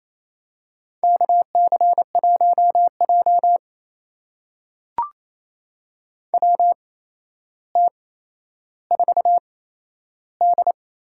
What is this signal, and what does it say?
Morse code, heard as a keyed tone, about 28 wpm, spaced as for 12 wpm, 700 Hz. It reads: KC1J WT4D
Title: USA Call Signs - 4 Characters - 8x Character Spacing - 28wpm